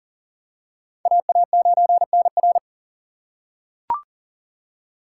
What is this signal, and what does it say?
Morse code, heard as a keyed tone, about 39 wpm, 700 Hz. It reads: AA9NR
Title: USA Call Signs - 5 Characters - 4x Character Spacing - 40wpm